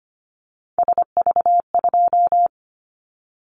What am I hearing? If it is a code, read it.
Morse code, heard as a keyed tone, about 25 wpm, 700 Hz. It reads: SV2
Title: Callsign Prefixes for Top 50 DXCC - 25wpm